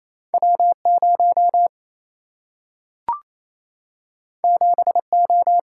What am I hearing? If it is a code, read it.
Morse code, heard as a keyed tone, about 28 wpm, 700 Hz. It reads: W 0 7 O
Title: Single-Letter Single-Number 28wpm